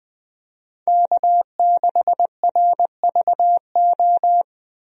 Morse code, heard as a keyed tone, about 21 wpm, 700 Hz. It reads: K6RVO
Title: USA Call Signs - 5 Characters - Repeated 3x - 20wpm